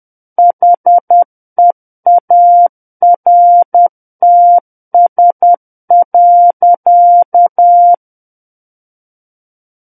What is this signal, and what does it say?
Morse code, heard as a keyed tone, about 10 wpm, 700 Hz. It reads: HEARTS.
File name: Book One / 01-Chapter 1 - 10 WPM